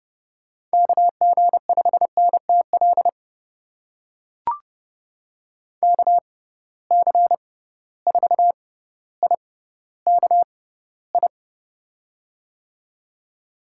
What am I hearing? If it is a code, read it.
Morse code, heard as a keyed tone, about 31 wpm, spaced as for 16 wpm, 700 Hz. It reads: KG5NTL KC4IKI
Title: USA Call Signs - 6 Characters - 6x Character Spacing - 30wpm